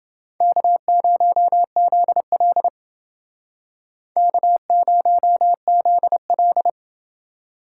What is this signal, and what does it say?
Morse code, heard as a keyed tone, about 30 wpm, 700 Hz. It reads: K0ZL K0ZL